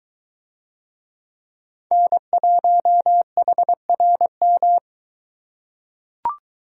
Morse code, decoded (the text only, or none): N1HRM